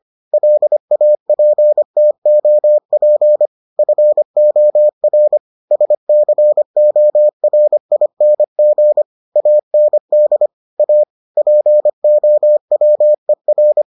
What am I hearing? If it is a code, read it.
Morse code, heard as a keyed tone, about 25 wpm, 600 Hz. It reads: LAPTOP FOR SCORING AND A POWER